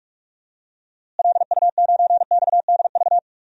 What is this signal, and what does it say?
Morse code, heard as a keyed tone, about 42 wpm, 700 Hz. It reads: LU9XDV
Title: DXCC-100 Call Signs - 6 Characters - Repeated 3x - 45wpm